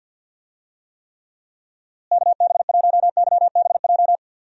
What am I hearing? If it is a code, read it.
Morse code, heard as a keyed tone, about 51 wpm, 700 Hz. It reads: KB1YBJ
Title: USA Call Signs - 6 Characters - Repeated 2x - 50wpm